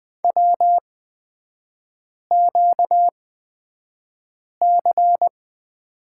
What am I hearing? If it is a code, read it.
Morse code, heard as a keyed tone, about 20 wpm, spaced as for 5 wpm, 690 Hz. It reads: WQC
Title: 236 Lesson 38e - Letter q - Three Characters 20wpm